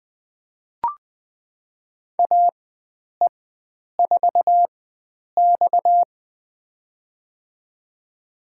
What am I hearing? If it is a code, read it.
Morse code, heard as a keyed tone, about 20 wpm, spaced as for 9 wpm, 700 Hz. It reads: AE4X
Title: USA Call Signs - 4 Characters - 4x Character Spacing - 20wpm